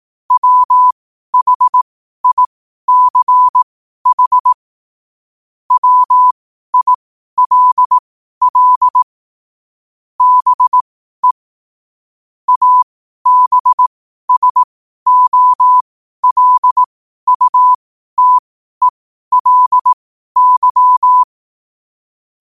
Morse code, heard as a keyed tone, about 18 wpm, spaced as for 12 wpm, 1000 Hz. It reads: WHICH WILL BE ABSOLUTELY
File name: qotdfiles / 12wpm